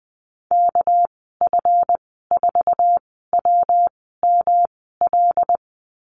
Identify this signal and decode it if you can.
Morse code, heard as a keyed tone, about 20 wpm, spaced as for 14 wpm, 700 Hz. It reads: KF4WML
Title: 219 Lesson 35g - Number 0 - Call Signs 20wpm